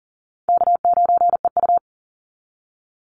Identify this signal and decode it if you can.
Morse code, heard as a keyed tone, about 40 wpm, 700 Hz. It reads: K9EU
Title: Easy Call Signs - 40wpm